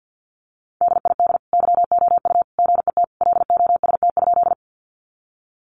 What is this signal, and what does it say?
Morse code, heard as a keyed tone, about 50 wpm, 700 Hz. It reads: DID YOU GET LOST?
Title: Sentences from Top 1400 Words 50wpm